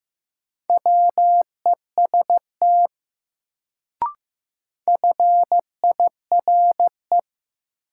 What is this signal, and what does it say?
Morse code, heard as a keyed tone, about 15 wpm, 700 Hz. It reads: WEST FIRE
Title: Sentences from Top 400 Words - New Word Drill - 15wpm